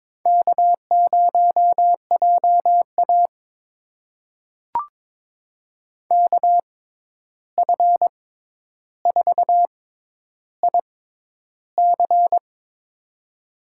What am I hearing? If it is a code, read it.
Morse code, heard as a keyed tone, about 22 wpm, spaced as for 11 wpm, 700 Hz. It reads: K0JA KF4IC
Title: USA Call Signs - 6x Character Spacing - 22wpm